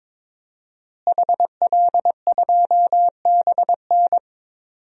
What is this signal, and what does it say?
Morse code, heard as a keyed tone, about 22 wpm, 700 Hz. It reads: HL2BN
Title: DXCC-100 Call Signs - 5 Characters - Repeated 2x - 22wpm